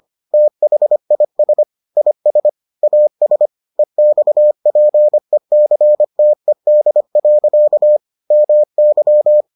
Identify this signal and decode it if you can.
Morse code, heard as a keyed tone, about 25 wpm, 600 Hz. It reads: THIS IS AS EXPECTED. MY